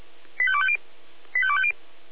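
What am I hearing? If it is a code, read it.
Five-tone selective call (ZVEI), heard as quick runs of five tones, tones of about 70 ms. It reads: 85250 85250
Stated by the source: ZVEI five-tone call 85253 received on SDR, Feuerwehr Orlamünde server.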